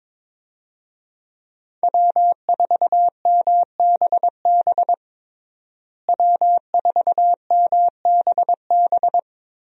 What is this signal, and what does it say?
Morse code, heard as a keyed tone, about 22 wpm, 700 Hz. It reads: W4MBB W4MBB
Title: USA Call Signs - 5 Characters - Repeated 2x - 22wpm